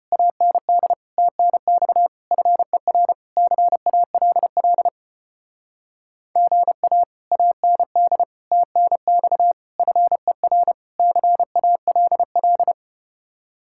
Morse code, heard as a keyed tone, about 33 wpm, 700 Hz. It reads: AND TNX FER CALL GA AND TNX FER CALL